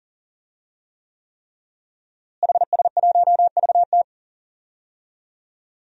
Morse code, heard as a keyed tone, about 38 wpm, 700 Hz. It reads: HS1VT
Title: DXCC-100 Call Signs - 5 Characters - Repeated 3x - 40wpm